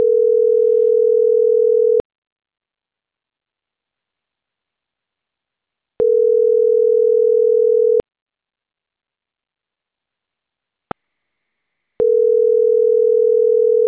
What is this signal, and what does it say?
Ringback tone (North America), 440 + 480 Hz.